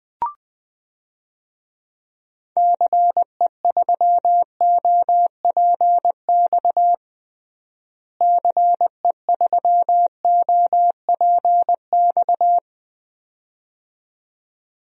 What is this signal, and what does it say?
Morse code, heard as a keyed tone, about 20 wpm, 700 Hz. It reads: CE3OPX CE3OPX
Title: DXCC-100 Call Signs - 6 Characters - Repeated 2x - 20wpm